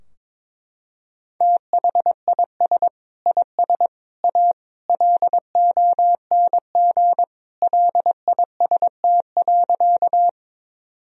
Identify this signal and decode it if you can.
Morse code, heard as a keyed tone, about 22 wpm, 700 Hz. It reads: THIS IS A LONG LIST.
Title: Sentences from Top 1000 Words - Review Part 2 - 22wpm